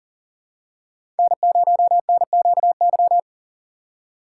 Morse code, heard as a keyed tone, about 40 wpm, 700 Hz. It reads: N0NQY
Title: USA Call Signs - 4x Character Spacing - 40wpm